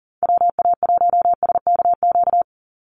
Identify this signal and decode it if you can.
Morse code, heard as a keyed tone, about 37 wpm, 700 Hz. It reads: WA1SKQ
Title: Call Signs - 40wpm